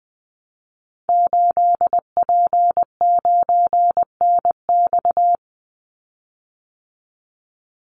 Morse code, heard as a keyed tone, about 20 wpm, 700 Hz. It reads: 8P9NX
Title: Call Signs Repeated 2x - 20wpm